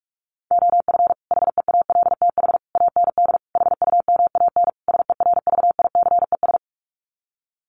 Morse code, heard as a keyed tone, about 44 wpm, 700 Hz. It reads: OF HEALTH AND HUMAN SERVICES
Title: Sets of 5 Words 45wpm